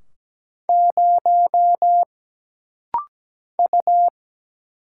Morse code, heard as a keyed tone, about 17 wpm, 700 Hz. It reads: 0 U
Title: Single Letter-Number Alt - Rapid-Fire 17wpm